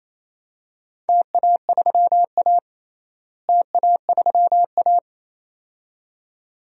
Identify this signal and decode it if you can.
Morse code, heard as a keyed tone, about 28 wpm, 700 Hz. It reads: TA3A TA3A
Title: DXCC-100 Call Signs - 4 Characters - Repeated 2x - 28wpm